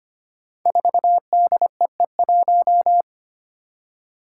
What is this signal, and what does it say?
Morse code, heard as a keyed tone, about 25 wpm, 700 Hz. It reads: 4DEE1
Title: Five Challenging Letter-Number Combinations 25wpm